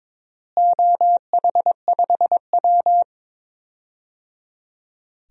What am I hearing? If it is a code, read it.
Morse code, heard as a keyed tone, about 22 wpm, 700 Hz. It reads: OH5W
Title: DXCC-100 Call Signs - 4 Characters - Repeated 3x - 22wpm